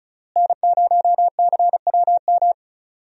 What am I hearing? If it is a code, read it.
Morse code, heard as a keyed tone, about 34 wpm, 700 Hz. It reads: N0CWM
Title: USA Call Signs - 5 Characters - Repeated 3x - 35wpm